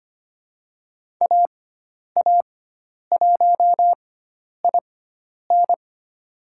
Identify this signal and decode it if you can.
Morse code, heard as a keyed tone, about 25 wpm, spaced as for 10 wpm, 700 Hz. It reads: AA1IN
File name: Lesson 16g - Letter c - Call Signs 25wpm